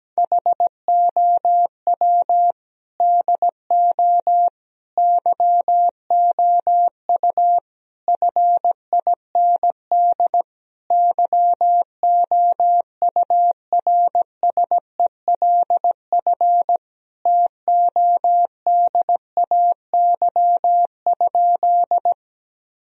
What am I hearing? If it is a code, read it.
Morse code, heard as a keyed tone, about 17 wpm, 700 Hz. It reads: HOW DO YOU FIND YOURSELF TODAY?
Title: Sentences from Top 1000 Words - Review Part 4 - 17wpm